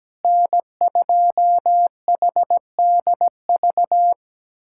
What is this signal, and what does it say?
Morse code, heard as a keyed tone, about 17 wpm, 700 Hz. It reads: N2HDV
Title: SOTA QSO Practice 17wpm